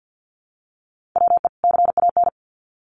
Morse code, heard as a keyed tone, about 50 wpm, 700 Hz. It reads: WE CAN
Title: Sets of 2 Words 50wpm